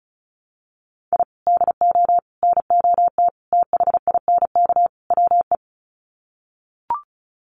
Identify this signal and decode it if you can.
Morse code, heard as a keyed tone, about 34 wpm, 700 Hz. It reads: I DO NOT THINK WE
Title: Sets of 5 Words 35wpm